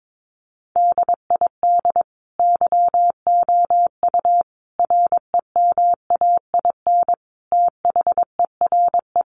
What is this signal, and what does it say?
Morse code, heard as a keyed tone, about 22 wpm, 700 Hz. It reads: DID YOU REMAIN THERE
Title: Sentences from Top 1200 Words 22wpm